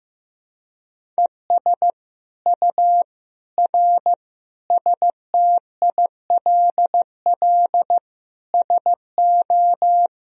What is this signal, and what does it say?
Morse code, heard as a keyed tone, about 15 wpm, 700 Hz. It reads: ES U R STILL SO
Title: Sentences from Top 400 Words and Abbreviations 15wpm